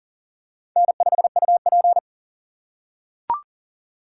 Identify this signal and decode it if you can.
Morse code, heard as a keyed tone, about 40 wpm, 700 Hz. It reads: N5UP
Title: USA Call Signs - 4 Characters - 4x Character Spacing - 40wpm